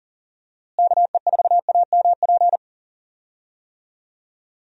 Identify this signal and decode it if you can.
Morse code, heard as a keyed tone, about 40 wpm, 700 Hz. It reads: KE4AMP
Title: USA Call Signs - 6 Characters - Repeated 2x - 40wpm